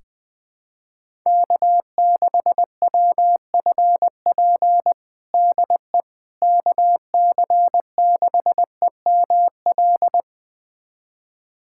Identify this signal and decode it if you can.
Morse code, heard as a keyed tone, about 20 wpm, 700 Hz. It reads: K6WFP DE KC6EML
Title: USA Callsign DE Callsign 20wpm